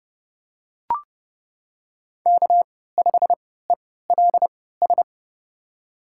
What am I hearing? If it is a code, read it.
Morse code, heard as a keyed tone, about 30 wpm, spaced as for 17 wpm, 700 Hz. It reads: K5ELS
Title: USA Call Signs - 5 Characters - 3x Character Spacing - 30wpm